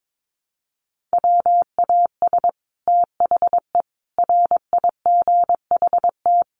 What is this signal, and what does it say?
Morse code, heard as a keyed tone, about 22 wpm, 700 Hz. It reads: WAS THE RIGHT
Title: Sets of 5 Words Encore2 22wpm